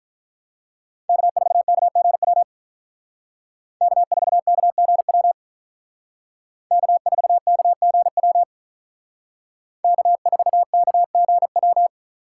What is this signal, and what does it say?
Morse code, heard as a keyed tone, about 44 wpm, 700 Hz. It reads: K4KGW K4KGW K4KGW K4KGW